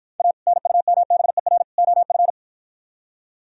Morse code, heard as a keyed tone, about 52 wpm, 700 Hz. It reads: A NUMBER OF